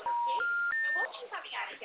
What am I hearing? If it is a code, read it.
Special information tone, with speech.